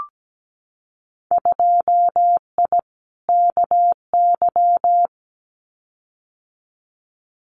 Morse code, heard as a keyed tone, about 17 wpm, 700 Hz. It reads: 2I KY
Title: Winter Field Day Exchanges 17wpm